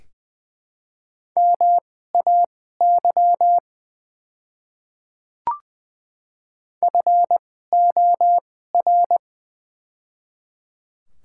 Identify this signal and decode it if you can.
Morse code, heard as a keyed tone, about 21 wpm, spaced as for 15 wpm, 700 Hz. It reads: MAY FOR
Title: 2x Character Spacing - Top 300 Words - 20wpm